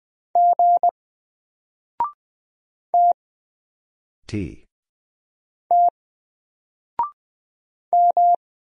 Morse code, heard as a keyed tone, about 20 wpm, 700 Hz. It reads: G T T M
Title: Single Letter-Number 20wpm